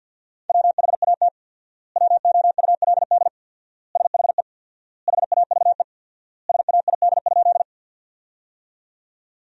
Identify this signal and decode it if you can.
Morse code, heard as a keyed tone, about 50 wpm, 700 Hz. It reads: WHAT WOULD SHE HAVE SAID?
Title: 4x Word-Spacing - Sentences from Top 100 Words - 50wpm